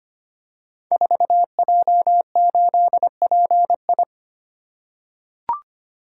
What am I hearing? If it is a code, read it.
Morse code, heard as a keyed tone, about 25 wpm, 700 Hz. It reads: 4J8PI
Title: Code Groups - 5 Letter:Numbers - 25wpm